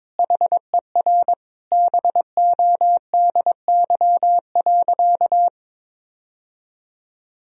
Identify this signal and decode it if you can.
Morse code, heard as a keyed tone, about 21 wpm, 700 Hz. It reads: HER BODY.
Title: Sentences from Top 1000 Words - Review Part 2 - 22wpm